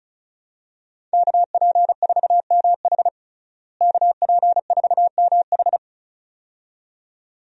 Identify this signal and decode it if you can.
Morse code, heard as a keyed tone, about 34 wpm, 700 Hz. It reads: KP4MH KP4MH
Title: DXCC-100 Call Signs - 5 Characters - Repeated 2x - 35wpm